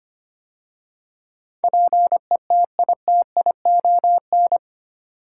Morse code, heard as a keyed tone, about 25 wpm, 700 Hz. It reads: PETITION
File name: 8 Letter Words - First 3 Letters Given - 25wpm